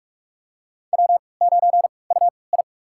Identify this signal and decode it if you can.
Morse code, heard as a keyed tone, about 44 wpm, spaced as for 25 wpm, 700 Hz. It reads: W9UI
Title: USA Call Signs - 3x Character Spacing - 45wpm